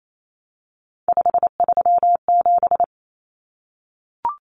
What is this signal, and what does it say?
Morse code, heard as a keyed tone, about 28 wpm, 700 Hz. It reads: 537
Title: Numbers - 3-Digit 28wpm